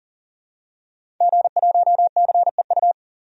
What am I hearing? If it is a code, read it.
Morse code, heard as a keyed tone, about 40 wpm, 700 Hz. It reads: G1CEU
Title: DXCC-100 Call Signs - 5 Characters - 40wpm